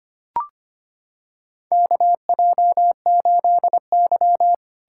Morse code, heard as a keyed tone, about 25 wpm, 700 Hz. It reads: KJ8Y
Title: USA Callsign DE Callsign - 4 Characters - 25wpm